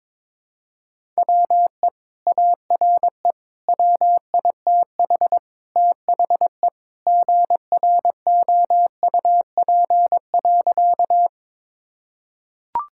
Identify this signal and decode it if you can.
Morse code, heard as a keyed tone, about 22 wpm, 700 Hz. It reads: WE ARE WITH THE GROUP.
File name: practice / 22wpm / Sentences with 2-Word Phrases - 22wpm